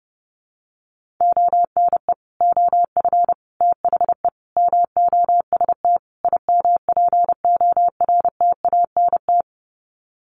Morse code, heard as a keyed tone, about 30 wpm, 700 Hz. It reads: ONE OF THE MOST IMPORTANT
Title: Sets of 5 Words 30wpm